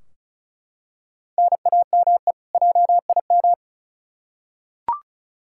Morse code, heard as a keyed tone, about 36 wpm, 700 Hz. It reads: NAME JIM
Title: K1USN SST QSO Practice 35wpm